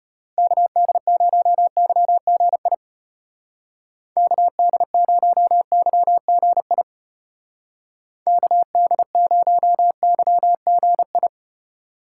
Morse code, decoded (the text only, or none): KD0YGI KD0YGI KD0YGI